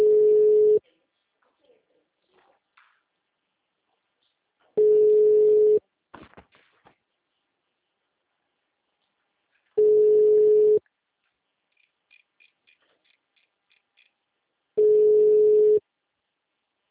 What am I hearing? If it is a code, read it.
Ringback tone (Europe, 425 Hz), with speech.